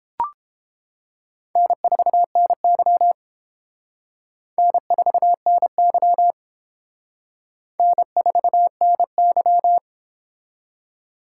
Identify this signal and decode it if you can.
Morse code, heard as a keyed tone, about 31 wpm, 700 Hz. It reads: N4NY N4NY N4NY